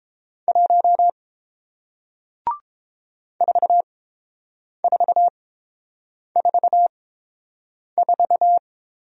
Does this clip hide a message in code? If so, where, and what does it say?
Yes: Morse code from 0.5 to 8.6 s, heard as a keyed tone, about 30 wpm, spaced as for 7 wpm, 700 Hz. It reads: 1 4444